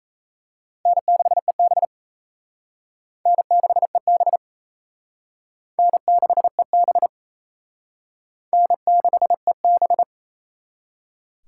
Morse code, heard as a keyed tone, about 34 wpm, 700 Hz. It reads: N6EB N6EB N6EB N6EB
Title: Speed Racing - USA 1x2 Call Signs 28wpm